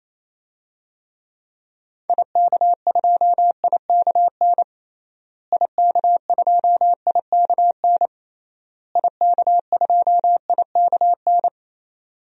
Morse code, heard as a keyed tone, about 28 wpm, 700 Hz. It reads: IK2IKN IK2IKN IK2IKN
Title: DXCC-100 Call Signs - 6 Characters - Repeated 3x - 28wpm